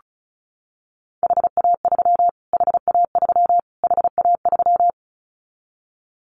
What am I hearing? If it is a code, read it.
Morse code, heard as a keyed tone, about 34 wpm, 700 Hz. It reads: HA3 HA3 HA3
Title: Callsign Prefixes for Top 25 DXCC - Repeated 3x - 35wpm